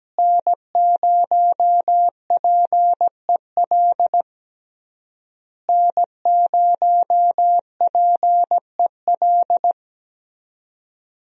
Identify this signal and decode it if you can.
Morse code, heard as a keyed tone, about 17 wpm, 700 Hz. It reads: N0PEL N0PEL
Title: USA Call Signs - 5 Characters - Repeated 3x - 17wpm